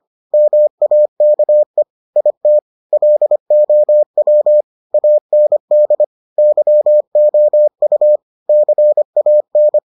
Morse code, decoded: MAKE IT LOW AND YOU CAN

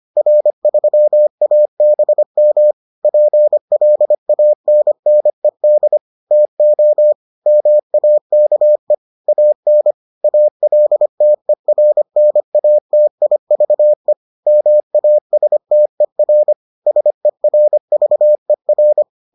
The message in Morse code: R3ABM PLANNED TO MAKE AN ALTERNATIVE MASTER SERVER

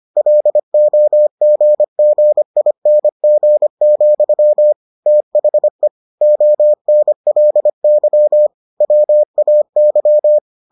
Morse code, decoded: LOGGING, THE ONLY WAY

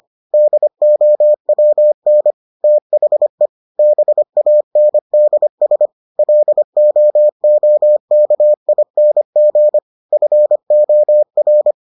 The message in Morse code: DOWN THE BANDS LOOKING FOR

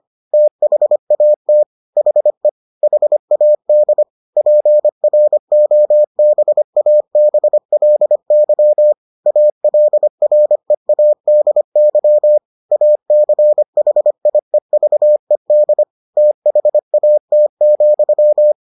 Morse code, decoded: THAT HE HAD PROBABLY ALREADY ACHIEVED THAT,

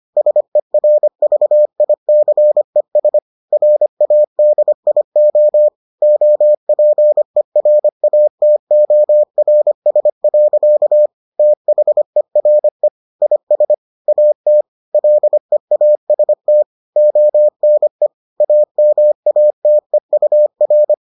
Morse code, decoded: SERVICES RADIO OPERATORS. THERE IS AT LEAST ONE AMATEUR

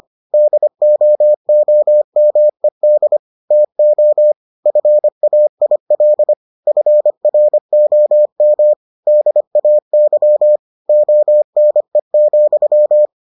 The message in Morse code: DOOMED TO FAIL FROM DAY ONE,

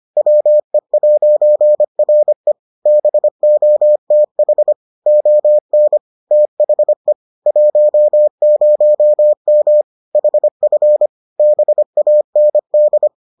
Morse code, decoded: WE'RE BOTH ON THE 10M HF BAND

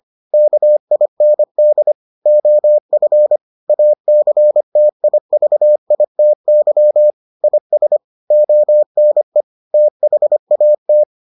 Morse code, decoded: KIND OF ACTIVITY IS ONE THAT

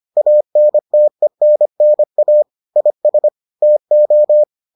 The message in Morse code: ANTENNA IS TO

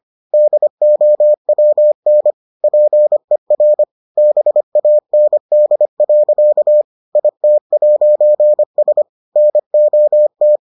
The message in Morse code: DOWN PER BAND. IT'S NOT

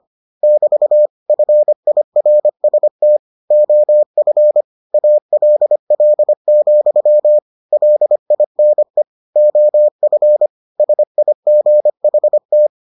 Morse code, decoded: <BT> FIRST OF ALL, LINE OF SIGHT

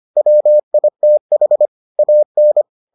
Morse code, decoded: WITH AN

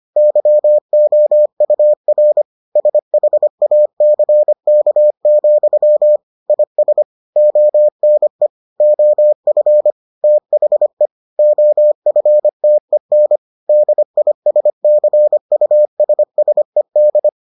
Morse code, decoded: YOUR SHACK, IS ONE OF THE OFTEN DISCUSSED

